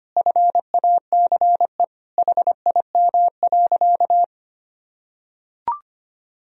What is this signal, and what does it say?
Morse code, heard as a keyed tone, about 25 wpm, 700 Hz. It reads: FACE HIM.
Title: Sentences from Top 1000 Words - Review Part 2 - 25wpm